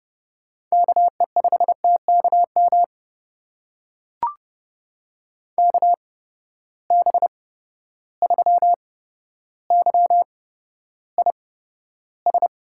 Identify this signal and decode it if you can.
Morse code, heard as a keyed tone, about 30 wpm, spaced as for 13 wpm, 700 Hz. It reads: KE5TKM KB3YIS
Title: USA Call Signs - 6 Characters - 8x Character Spacing - 30wpm